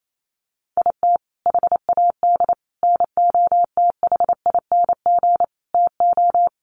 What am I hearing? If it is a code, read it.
Morse code, heard as a keyed tone, about 28 wpm, 700 Hz. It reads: IT HAD NOTHING TO
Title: Sets of 5 Words Encore2 28wpm